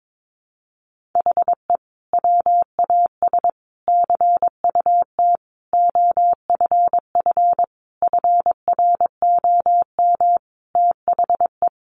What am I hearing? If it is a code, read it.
Morse code, heard as a keyed tone, about 22 wpm, 700 Hz. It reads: HE WAS CUT OFF FROM THE
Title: Sentences from Top 1100 Words 22wpm